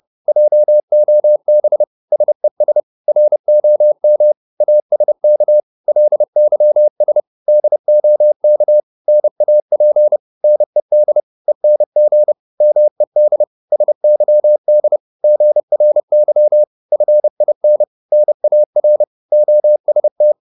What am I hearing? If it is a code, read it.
Morse code, heard as a keyed tone, about 30 wpm, 600 Hz. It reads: JOB SES ROM ASK LYS DOK NAP NED ENG MED SYD GRY FIN NAR OST
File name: Korte ord DK 30wpm